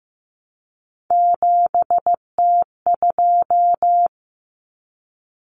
Morse code, heard as a keyed tone, about 15 wpm, 700 Hz. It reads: 7T2
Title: Common Cut Numbers - 3-Digit 15wpm